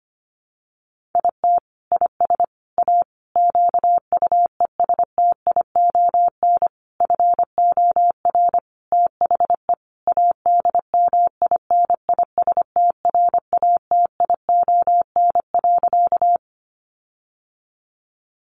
Morse code, heard as a keyed tone, about 25 wpm, 700 Hz. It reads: IT IS A QUESTION FOR THE ADMINISTRATION.
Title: Sentences from Top 1300 Words 25wpm